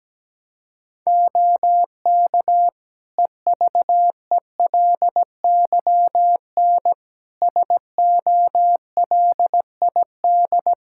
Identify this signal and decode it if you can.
Morse code, heard as a keyed tone, about 17 wpm, 700 Hz. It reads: OK EVELYN SOLID
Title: Practice QSO - Exchange 3 - More Info - 17wpm